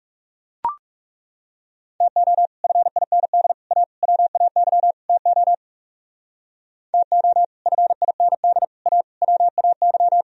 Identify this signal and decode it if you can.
Morse code, heard as a keyed tone, about 37 wpm, 700 Hz. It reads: TO FIND A WAY TO TO FIND A WAY
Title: Speed-Racing - Sets of 5 Words 30wpm